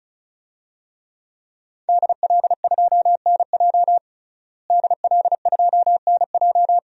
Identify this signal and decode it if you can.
Morse code, heard as a keyed tone, about 36 wpm, 700 Hz. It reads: DL2NJ DL2NJ
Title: DXCC-100 Call Signs - 5 Characters - Repeated 2x - 35wpm